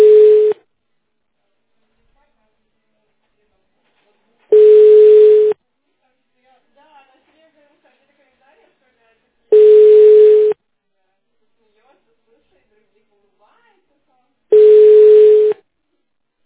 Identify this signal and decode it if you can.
Ringback tone (Europe, 425 Hz), with speech.